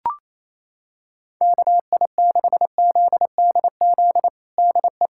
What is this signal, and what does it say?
Morse code, heard as a keyed tone, about 27 wpm, 700 Hz. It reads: KI6ZDZ DE
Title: USA Callsign DE Callsign 28wpm